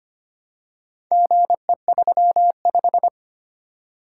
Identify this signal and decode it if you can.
Morse code, heard as a keyed tone, about 25 wpm, 700 Hz. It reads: GE35
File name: practice / 25wpm / Four Letter-Number Combinations - 25wpm